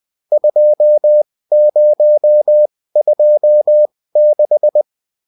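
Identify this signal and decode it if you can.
Morse code, heard as a keyed tone, about 20 wpm, spaced as for 16 wpm, 600 Hz. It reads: 2026
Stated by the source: Morse code bulletin of a French amateur CW union, sent as text and in CW.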